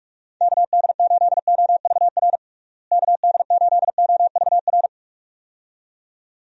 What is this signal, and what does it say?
Morse code, heard as a keyed tone, about 44 wpm, 700 Hz. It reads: KD8OVR KD8OVR
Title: USA Call Signs - 6 Characters - Repeated 2x - 45wpm